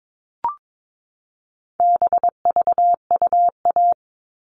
Morse code, heard as a keyed tone, about 22 wpm, 700 Hz. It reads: BVUA
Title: All Cut Numbers - 4-Digit 22wpm